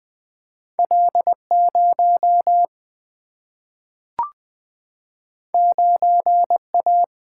Morse code, heard as a keyed tone, about 20 wpm, 700 Hz. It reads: L0 9A
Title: Single-Letter Single-Number 20wpm